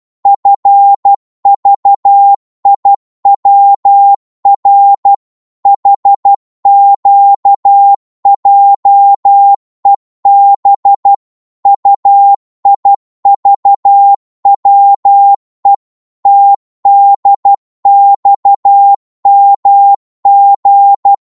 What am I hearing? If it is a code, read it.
Morse code, heard as a keyed tone, about 12 wpm, 800 Hz. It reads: FVIWR HQJEB UIVWE TDXMG
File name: Morse / ltr-12wpm